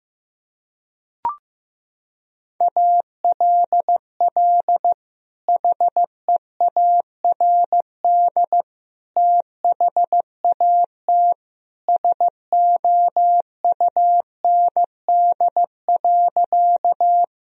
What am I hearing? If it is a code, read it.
Morse code, heard as a keyed tone, about 16 wpm, 700 Hz. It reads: ALL HEARD THAT SOUND.
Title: Sentences from Top 1000 Words - Review Part 1 - 15wpm